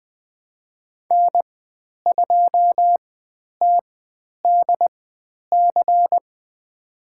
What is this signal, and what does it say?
Morse code, heard as a keyed tone, about 20 wpm, spaced as for 10 wpm, 700 Hz. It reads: N2TDC
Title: 095 Lesson 16g - Letter c - Call Signs 20wpm